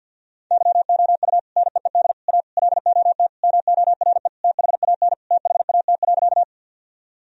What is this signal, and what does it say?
Morse code, heard as a keyed tone, about 50 wpm, 700 Hz. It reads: YOU NEED A LOT MORE THAN THAT.
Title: Sentences from Top 1000 Words - Review Part 3 - 50wpm